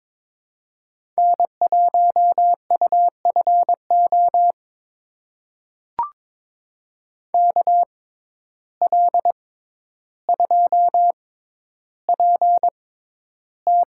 Morse code, heard as a keyed tone, about 22 wpm, spaced as for 11 wpm, 700 Hz. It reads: N1UFO KL2PT
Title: USA Call Signs - 5 Characters - 6x Character Spacing - 22wpm